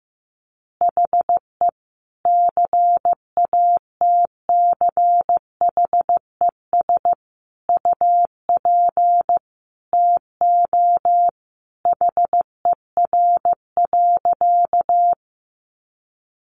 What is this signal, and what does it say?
Morse code, heard as a keyed tone, about 15 wpm, 700 Hz. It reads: HE CATCHES UP TO HER.
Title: Additional Easy Phrases 15wpm